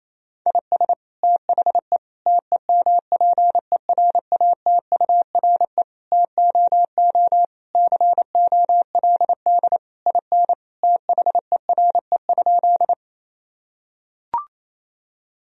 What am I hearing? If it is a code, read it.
Morse code, heard as a keyed tone, about 28 wpm, 700 Hz. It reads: IS THE TEMPERATURE TOO COLD IN THERE?
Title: Sentences from Top 1000 Words - Review Part 4 - 28wpm